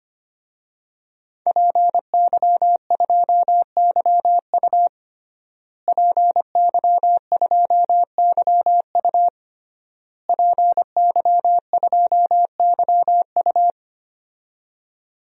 Morse code, heard as a keyed tone, about 25 wpm, 700 Hz. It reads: PY2YU PY2YU PY2YU
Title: DXCC-100 Call Signs - 5 Characters - Repeated 3x - 25wpm